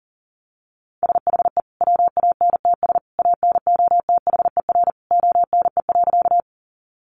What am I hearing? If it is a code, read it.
Morse code, heard as a keyed tone, about 40 wpm, 700 Hz. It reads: SHE WANTS ANOTHER ONE.
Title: Easy Phrases 40wpm